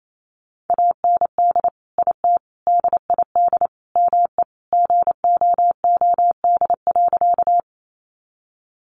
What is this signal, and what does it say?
Morse code, heard as a keyed tone, about 28 wpm, 700 Hz. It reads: AND IT DID ME GOOD.
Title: Sentences from Top 200 Words 28wpm